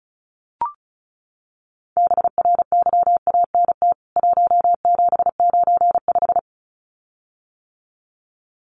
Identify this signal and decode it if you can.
Morse code, heard as a keyed tone, about 34 wpm, 700 Hz. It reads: BRYANT 1795
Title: CWT Exchange 35wpm